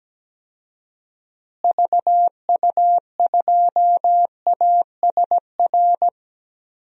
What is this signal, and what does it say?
Morse code, heard as a keyed tone, about 17 wpm, 700 Hz. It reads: VU2ASR
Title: DXCC-100 Call Signs - 6 Characters - Repeated 2x - 17wpm